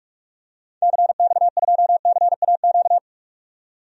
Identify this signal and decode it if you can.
Morse code, heard as a keyed tone, about 45 wpm, 700 Hz. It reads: CX2CAQ
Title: DXCC-100 Call Signs - 6 Characters - 45wpm